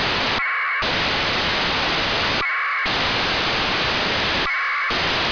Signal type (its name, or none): packet radio (AX.25, as in APRS)